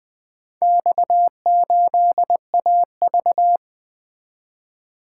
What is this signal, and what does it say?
Morse code, heard as a keyed tone, about 20 wpm, 700 Hz. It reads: X8AV
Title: Four Letter-Number Combinations - 20wpm